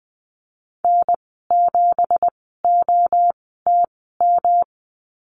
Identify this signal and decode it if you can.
Morse code, heard as a keyed tone, about 20 wpm, spaced as for 15 wpm, 700 Hz. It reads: N7OTM
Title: 193 Lesson 31g - Letter v - Call Signs 20wpm